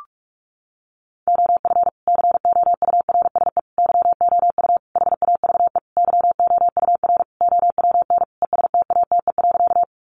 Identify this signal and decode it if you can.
Morse code, heard as a keyed tone, about 44 wpm, 700 Hz. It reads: OF COURSE YOU HAVE YOUR OWN ESTATE.
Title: Sentences from Top 1200 Words 45wpm